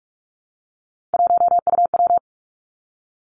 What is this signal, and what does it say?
Morse code, heard as a keyed tone, about 46 wpm, 700 Hz. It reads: 1UW
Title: Three Challenging Letter-Number Combinations 45wpm